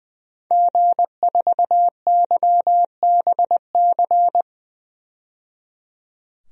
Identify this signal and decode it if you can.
Morse code, heard as a keyed tone, about 21 wpm, 700 Hz. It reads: G4YBC
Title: DXCC-100 Call Signs - 5 Characters - 20wpm